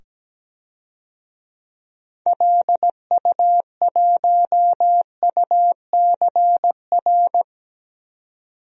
Morse code, heard as a keyed tone, about 17 wpm, 700 Hz. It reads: LU1UCR
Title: DXCC-100 Call Signs - 6 Characters - 17wpm